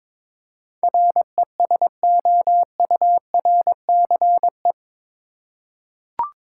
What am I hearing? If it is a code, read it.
Morse code, heard as a keyed tone, about 22 wpm, 700 Hz. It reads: RESOURCE
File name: practice / 22wpm / Eight Letter Words - 22wpm